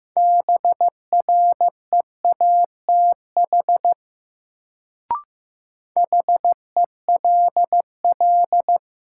Morse code, heard as a keyed tone, about 15 wpm, 700 Hz. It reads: BREATH HELL
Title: Sets of 5 Words Encore - New Word Drill - 15wpm